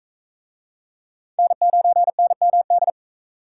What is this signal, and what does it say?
Morse code, heard as a keyed tone, about 36 wpm, 700 Hz. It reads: N9NMD